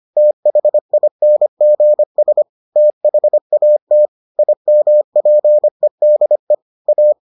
Morse code, heard as a keyed tone, about 25 wpm, 600 Hz. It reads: THINGS THAT IMPEDE A